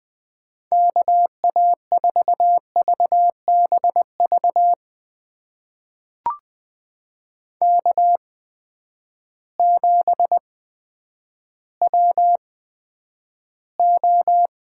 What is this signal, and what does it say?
Morse code, heard as a keyed tone, about 20 wpm, spaced as for 10 wpm, 700 Hz. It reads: KA4VBV K7WO
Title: USA Call Signs - 8x Character Spacing - 20wpm